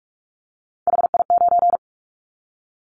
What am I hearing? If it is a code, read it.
Morse code, heard as a keyed tone, about 46 wpm, 700 Hz. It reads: HI9
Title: Callsign Prefixes for Top 100 DXCC - 45wpm